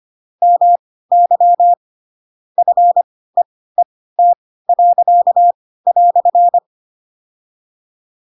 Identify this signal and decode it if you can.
Morse code, heard as a keyed tone, about 25 wpm, spaced as for 16 wpm, 700 Hz. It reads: MY FEET."
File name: Pinocchio / Pinocchio ch7 10 25 ik1hge